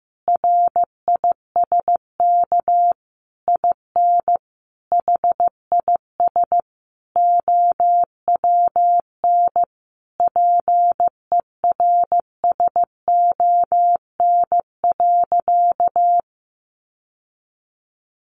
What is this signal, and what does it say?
Morse code, heard as a keyed tone, about 15 wpm, 700 Hz. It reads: RISK IN HIS OWN PERSON.
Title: Sentences from Top 600 Words 15wpm